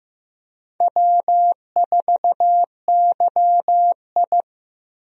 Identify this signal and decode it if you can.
Morse code, heard as a keyed tone, about 16 wpm, 700 Hz. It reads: W4YI
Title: USA Call Signs - 4 Characters - Repeated 2x - 15wpm